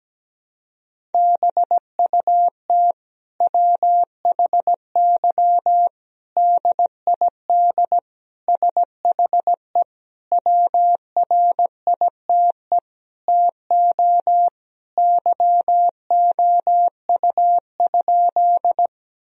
Morse code, heard as a keyed tone, about 17 wpm, 700 Hz. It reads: BUT WHY DID SHE WRITE TO YOU?
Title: Sentences from Top 1000 Words - Review Part 1 - 17wpm